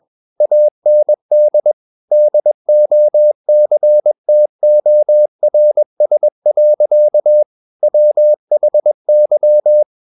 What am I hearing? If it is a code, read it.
Morse code, heard as a keyed tone, about 21 wpm, 600 Hz. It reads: AND DOCTORS. WHY
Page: CW med Gnister